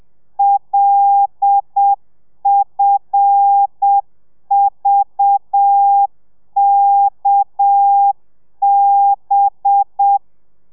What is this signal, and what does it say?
Morse code, heard as a keyed tone, about 7 wpm, 800 Hz. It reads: LFVKB